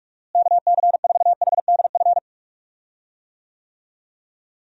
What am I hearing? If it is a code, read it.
Morse code, heard as a keyed tone, about 47 wpm, 700 Hz. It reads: KC4HDF
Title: USA Call Signs - 6 Characters - Repeated 3x - 45wpm